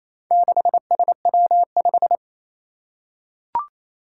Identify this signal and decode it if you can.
Morse code, heard as a keyed tone, about 27 wpm, 700 Hz. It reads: 6SW5
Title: Four Challenging Letter-Number Combinations 28wpm